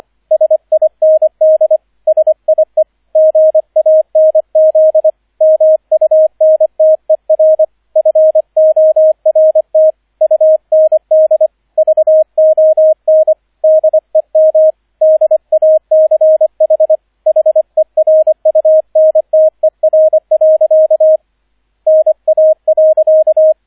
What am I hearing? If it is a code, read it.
Morse code, heard as a keyed tone, about 25 wpm, 630 Hz, with radio noise. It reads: SIND SIE GANZ MUNTER FORT UND VON DEM DACH HERUNTER. NA.